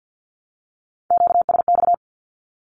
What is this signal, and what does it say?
Morse code, heard as a keyed tone, about 48 wpm, 700 Hz. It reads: QSX
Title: Q-Codes Repeated 3x 50wpm